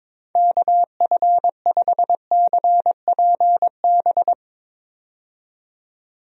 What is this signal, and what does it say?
Morse code, heard as a keyed tone, about 22 wpm, 700 Hz. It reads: KF5CPB
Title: USA Call Signs - 6 Characters - 22wpm